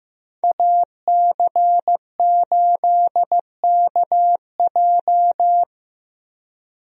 Morse code, heard as a keyed tone, about 15 wpm, 700 Hz. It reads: AC8KJ
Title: USA Call Signs - 5 Characters - 15wpm